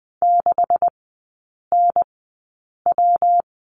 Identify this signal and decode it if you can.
Morse code, heard as a keyed tone, about 20 wpm, spaced as for 8 wpm, 700 Hz. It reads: 6NW
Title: 127 Lesson 21e - Number 6 - Three Characters 20wpm